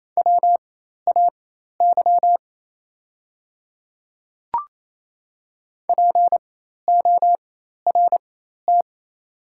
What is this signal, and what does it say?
Morse code, heard as a keyed tone, about 28 wpm, spaced as for 13 wpm, 700 Hz. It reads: WAY PORT
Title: 4x Character Spacing - Top 300 Words - 28wpm